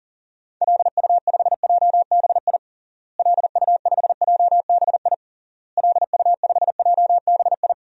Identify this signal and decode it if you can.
Morse code, heard as a keyed tone, about 40 wpm, 700 Hz. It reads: LU5JBI LU5JBI LU5JBI
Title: DXCC-100 Call Signs - 6 Characters - Repeated 3x - 40wpm